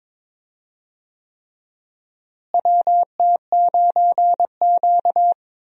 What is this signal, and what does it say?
Morse code, heard as a keyed tone, about 22 wpm, 700 Hz. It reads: WT9Q
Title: USA Call Signs - 4 Characters - Repeated 3x - 22wpm